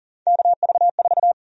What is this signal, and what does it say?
Morse code, heard as a keyed tone, about 40 wpm, 700 Hz. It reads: KV4
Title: USA Call Sign Prefixes - 40wpm